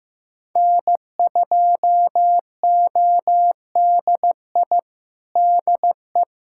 Morse code, heard as a keyed tone, about 15 wpm, 700 Hz. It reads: N2ODI DE
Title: Practice QSO - Exchange 4 - Ending - 15wpm